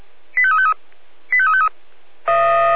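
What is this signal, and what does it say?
Five-tone selective call (ZVEI), heard as quick runs of five tones, tones of about 70 ms. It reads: 85353 85353
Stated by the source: ZVEI five-tone selective call 85878 received on SDR.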